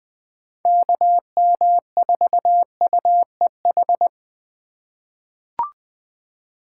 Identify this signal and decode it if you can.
Morse code, heard as a keyed tone, about 21 wpm, 700 Hz. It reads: KM4UEH
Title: USA Call Signs - 6 Characters - 4x Character Spacing - 20wpm